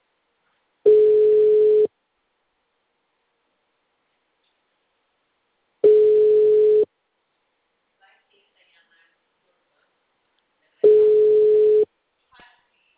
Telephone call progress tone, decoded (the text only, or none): ringback tone